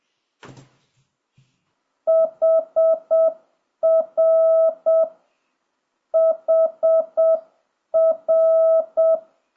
Morse code, heard as a keyed tone, about 7 wpm, 660 Hz. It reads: HR HR
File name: jh8cbh / 2026cw